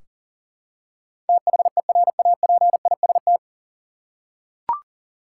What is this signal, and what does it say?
Morse code, heard as a keyed tone, about 40 wpm, 700 Hz. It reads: THERAPIST
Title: Practice QSO - Exchange 3 - More Info - 40wpm